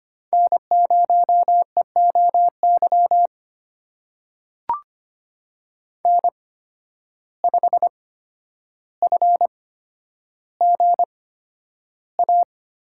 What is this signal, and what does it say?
Morse code, heard as a keyed tone, about 25 wpm, spaced as for 11 wpm, 700 Hz. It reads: N0EOY N5FGA
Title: USA Call Signs - 5 Characters - 8x Character Spacing - 25wpm